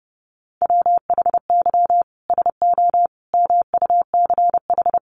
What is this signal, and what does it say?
Morse code, heard as a keyed tone, about 28 wpm, 700 Hz. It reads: WHY SO MUCH
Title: Sentences from Top 1200 Words 30wpm